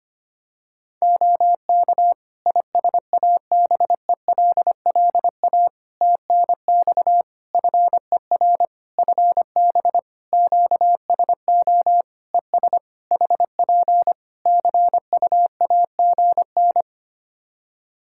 Morse code, heard as a keyed tone, about 25 wpm, 700 Hz. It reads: OK ISABELLA TNX FER FB QSO ES HP CUAGN
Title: Practice QSO - Exchange 4 - Ending - 25wpm